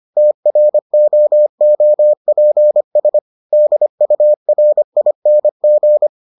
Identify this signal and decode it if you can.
Morse code, heard as a keyed tone, about 25 wpm, 600 Hz. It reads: TROOPS DURING